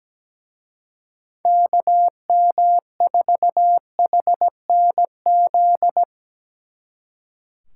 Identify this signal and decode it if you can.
Morse code, heard as a keyed tone, about 20 wpm, 700 Hz. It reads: KM4HNZ
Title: Speed Racing - USA Call Signs 17wpm